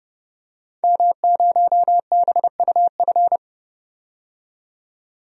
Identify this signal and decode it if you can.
Morse code, heard as a keyed tone, about 30 wpm, 700 Hz. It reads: M0BUF
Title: DXCC-100 Call Signs - 5 Characters - 30wpm